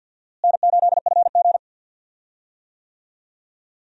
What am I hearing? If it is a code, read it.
Morse code, heard as a keyed tone, about 51 wpm, 700 Hz. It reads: N8FG
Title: USA Call Signs - 4 Characters - Repeated 2x - 50wpm